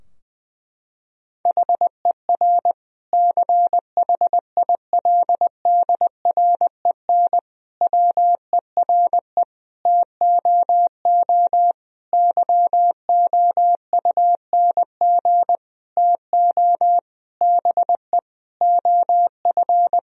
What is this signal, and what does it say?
Morse code, heard as a keyed tone, about 19 wpm, 700 Hz. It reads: HER CHILDREN WERE TOO YOUNG TO BE OF
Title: Sentences from Top 1000 Words - Review Part 4 - 20wpm